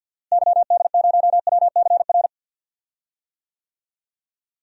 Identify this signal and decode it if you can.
Morse code, heard as a keyed tone, about 51 wpm, 700 Hz. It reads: YD0WCR
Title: DXCC-100 Call Signs - 6 Characters - Repeated 3x - 50wpm